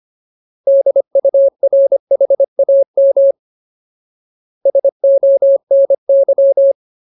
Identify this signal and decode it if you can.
Morse code, heard as a keyed tone, about 25 wpm, 550 Hz. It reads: DURHAM SONY
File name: Ultimate Word List / 25WPM Ultimate